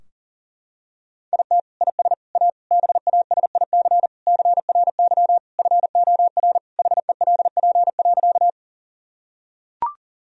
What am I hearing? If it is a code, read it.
Morse code, heard as a keyed tone, about 40 wpm, 700 Hz. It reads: IT IS A BASIC CRY FOR HELP.
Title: Sentences from Top 1000 Words - Review Part 3 - 40wpm